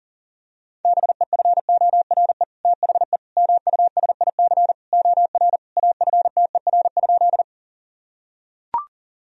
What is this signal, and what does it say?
Morse code, heard as a keyed tone, about 40 wpm, 700 Hz. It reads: BEFORE THE MUSIC OR AFTER?
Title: Sentences from Top 1000 Words - Review Part 1 - 40wpm